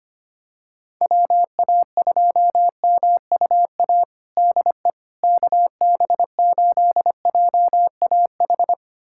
Morse code, heard as a keyed tone, about 25 wpm, 700 Hz. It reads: WA2MUA DE KB8JAH
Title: USA Callsign DE Callsign 25wpm